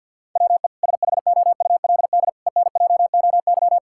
Morse code, heard as a keyed tone, about 50 wpm, 700 Hz. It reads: WE SHOULD ENJOY
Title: Sentences from Top 1000 Words - Review Part 5 - 50wpm